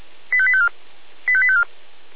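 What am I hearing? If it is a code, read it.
Five-tone selective call (ZVEI), heard as quick runs of five tones, tones of about 70 ms. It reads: 85854 85854
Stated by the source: Five-tone ZVEI selective call received on an SDR.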